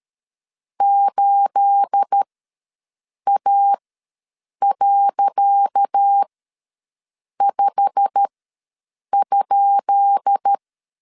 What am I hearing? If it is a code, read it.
Morse code, heard as a keyed tone, about 13 wpm, spaced as for 7 wpm, 800 Hz. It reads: 8A.5?
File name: LEZIONI / 10 SIGNS / 10C DOT ws 12wpm 3548